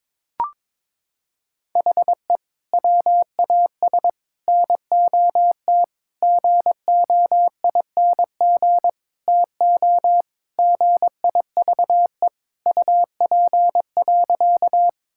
Morse code, heard as a keyed tone, about 22 wpm, 700 Hz. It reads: HE WAS NOT GOING TO GIVE UP.
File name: practice / 22wpm / Sentences with 4-Word Phrases - 22wpm